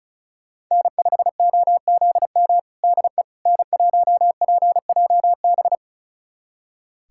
Morse code, heard as a keyed tone, about 36 wpm, 700 Hz. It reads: N5OZM DE N1PJB
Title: USA Callsign DE Callsign - 5 Characters - 35wpm